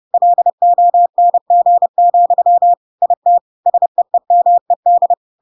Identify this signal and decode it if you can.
Morse code, heard as a keyed tone, about 30 wpm, 700 Hz. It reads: LONG, IT SEEMED
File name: Pinocchio / Pinocchio ch3 30 30 ik1hge